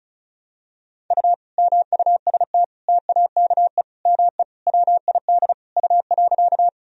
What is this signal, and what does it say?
Morse code, heard as a keyed tone, about 36 wpm, 700 Hz. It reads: U MUST TAKE ME WID U.
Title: Sentences from Top 200 Words and Abbreviations 35wpm